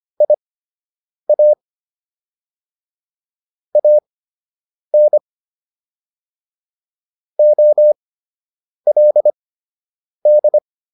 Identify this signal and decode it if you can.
Morse code, heard as a keyed tone, about 25 wpm, spaced as for 8 wpm, 600 Hz. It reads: IA AN OLD